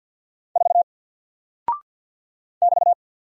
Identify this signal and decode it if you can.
Morse code, heard as a keyed tone, about 48 wpm, 700 Hz. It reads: 4 <BT>
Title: Single Character (Letter:Number:Punctuation:Prosign) 50wpm